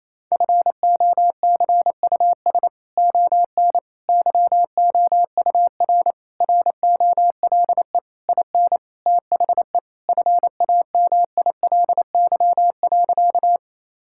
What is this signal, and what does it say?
Morse code, heard as a keyed tone, about 28 wpm, 700 Hz. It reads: FOCUS ON YOUR ROLE IN THE FAMILY.
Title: Sentences from Top 1000 Words - Review Part 6 - 28wpm